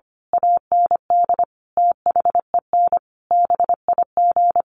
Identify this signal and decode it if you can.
Morse code, heard as a keyed tone, about 25 wpm, 700 Hz. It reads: AND THEN BIG
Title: A Sound Of Thunder 25wpm